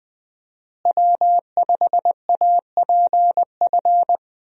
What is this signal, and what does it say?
Morse code, heard as a keyed tone, about 20 wpm, 700 Hz. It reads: W5APF
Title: USA Call Signs - 5 Characters - Repeated 3x - 20wpm